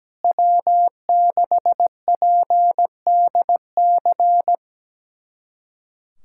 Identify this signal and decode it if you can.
Morse code, heard as a keyed tone, about 17 wpm, 700 Hz. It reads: W6PDC
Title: USA Callsign DE Callsign - 5 Characters - 17wpm